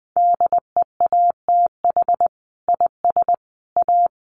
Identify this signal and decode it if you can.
Morse code, heard as a keyed tone, about 20 wpm, 700 Hz. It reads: DEATH IS A
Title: Sentences from Top 1500 Words 20wpm